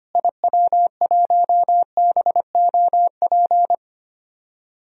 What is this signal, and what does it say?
Morse code, heard as a keyed tone, about 25 wpm, 700 Hz. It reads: IW1BOP
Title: DXCC-100 Call Signs - 6 Characters - Repeated 2x - 25wpm